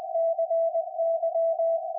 Morse code, heard as a keyed tone, about 20 wpm, 660 Hz, with radio noise. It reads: CY
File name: IC-7100